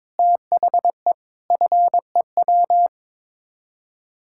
Morse code, heard as a keyed tone, about 22 wpm, 700 Hz. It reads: THE FEW
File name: practice / 22wpm / Build A Phrase - 5-Word Phrases - 22wpm